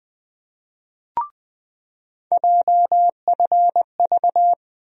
Morse code, heard as a keyed tone, about 20 wpm, 700 Hz. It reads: JFV
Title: USA Call Sign Postfixes - 3 Letters 20wpm